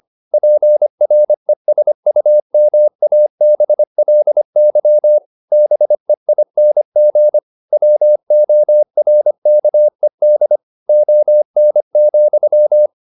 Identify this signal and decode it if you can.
Morse code, heard as a keyed tone, about 25 wpm, 600 Hz. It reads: PRESUMABLY BEING WORKED ON,